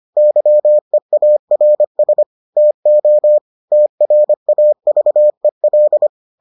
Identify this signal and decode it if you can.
Morse code, heard as a keyed tone, about 25 wpm, 600 Hz. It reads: YEARS TO TRAVEL